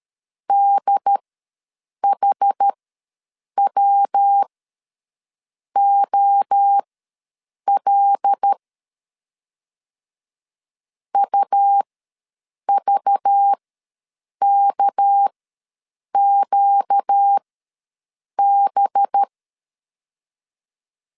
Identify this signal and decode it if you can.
Morse code, heard as a keyed tone, about 13 wpm, spaced as for 7 wpm, 800 Hz. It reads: DHWOL UVKQB
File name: LEZIONI / 6 PKX0 / 6 PK ws 12wpm 3116